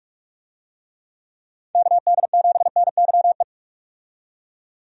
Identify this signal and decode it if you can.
Morse code, heard as a keyed tone, about 43 wpm, 700 Hz. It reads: KD7NYE
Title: USA Call Signs - 6 Characters - 45wpm